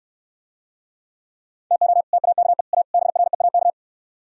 Morse code, heard as a keyed tone, about 68 wpm, 700 Hz. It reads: TO TAKE A BREAK